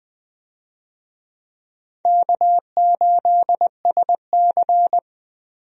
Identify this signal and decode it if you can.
Morse code, heard as a keyed tone, about 21 wpm, 700 Hz. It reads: K8SC
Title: USA Call Signs - 4 Characters - 20wpm